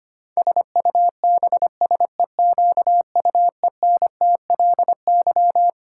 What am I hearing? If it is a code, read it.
Morse code, heard as a keyed tone, about 25 wpm, 700 Hz. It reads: SUBSEQUENTLY
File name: practice / 25wpm / Twelve Letter Words - 25wpm